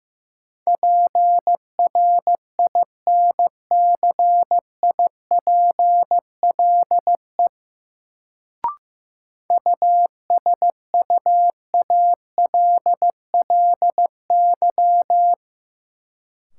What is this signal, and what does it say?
Morse code, heard as a keyed tone, about 15 wpm, 700 Hz. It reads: PRINCIPLE USUALLY
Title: Sentences from Top 1400 Words - New Word Drill Alt - 15wpm